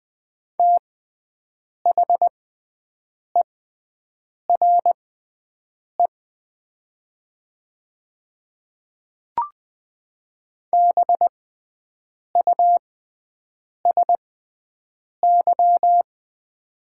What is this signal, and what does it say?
Morse code, heard as a keyed tone, about 20 wpm, spaced as for 7 wpm, 700 Hz. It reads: THERE BUSY
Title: 6x Character Spacing - Top 500 Words - 20wpm